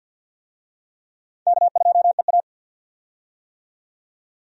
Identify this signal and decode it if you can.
Morse code, heard as a keyed tone, about 50 wpm, 700 Hz. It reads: K2EA
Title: USA Call Signs - 4 Characters - 50wpm